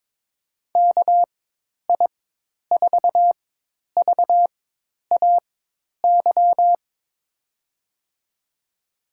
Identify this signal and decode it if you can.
Morse code, heard as a keyed tone, about 22 wpm, spaced as for 10 wpm, 700 Hz. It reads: KI4VAY
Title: USA Call Signs - 6 Characters - 4x Character Spacing - 22wpm